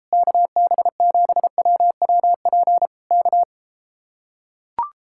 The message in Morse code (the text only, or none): KB7WWP K